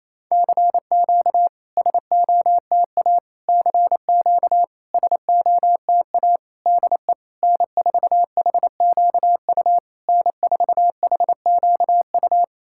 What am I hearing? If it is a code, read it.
Morse code, heard as a keyed tone, about 28 wpm, 700 Hz. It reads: CQ SOTA CQ SOTA DE N4HQU N4HQU
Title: SOTA QSO Practice 28wpm